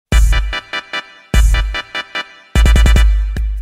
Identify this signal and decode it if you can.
There is no code.